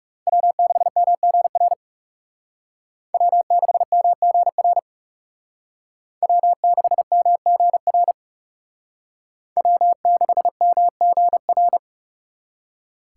Morse code, heard as a keyed tone, about 36 wpm, 700 Hz. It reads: W6MGR W6MGR W6MGR W6MGR
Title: Speed Racing - USA 1x3 Call Signs 30wpm